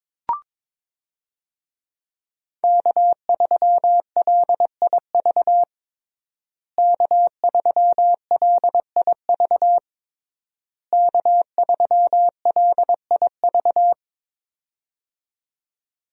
Morse code, heard as a keyed tone, about 22 wpm, 700 Hz. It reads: K3LIV K3LIV K3LIV
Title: USA Call Signs - 5 Characters - Repeated 3x - 22wpm